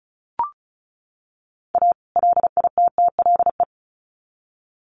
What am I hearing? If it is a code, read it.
Morse code, heard as a keyed tone, about 34 wpm, 700 Hz. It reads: A LITTLE
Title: Sets of 2 Words 35wpm